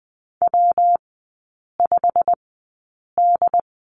Morse code, heard as a keyed tone, about 20 wpm, spaced as for 8 wpm, 700 Hz. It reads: W5D
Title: 114 Lesson 19e - Letter w - Three Characters 20wpm